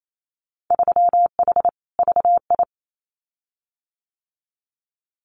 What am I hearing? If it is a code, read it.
Morse code, heard as a keyed tone, about 28 wpm, 700 Hz. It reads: 3H VI
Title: Winter Field Day Exchanges Encore 28wpm